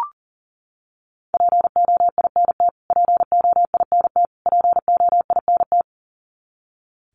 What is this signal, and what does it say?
Morse code, heard as a keyed tone, about 40 wpm, 700 Hz. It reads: POINT POINT POINT
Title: Top 500 Words Repeated 40wpm